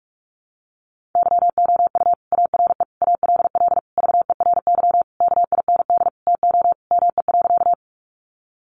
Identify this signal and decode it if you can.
Morse code, heard as a keyed tone, about 44 wpm, 700 Hz. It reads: YOU ARE ALL VERY KIND TO ME.
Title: Sentences from Top 200 Words 45wpm